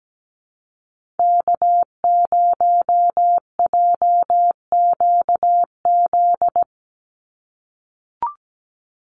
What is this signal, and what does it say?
Morse code, heard as a keyed tone, about 17 wpm, 700 Hz. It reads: K0JQZ
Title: NA SOTA Activator Callsigns 17wpm